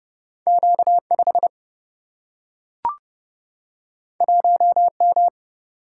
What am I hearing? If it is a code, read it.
Morse code, heard as a keyed tone, about 31 wpm, 700 Hz. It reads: Q 5 1 M
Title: Single-Letter Single-Number 30wpm